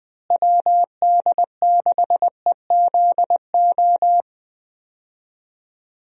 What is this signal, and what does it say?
Morse code, heard as a keyed tone, about 21 wpm, 700 Hz. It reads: WD6EZO
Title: USA Call Signs - 6 Characters - Repeated 3x - 20wpm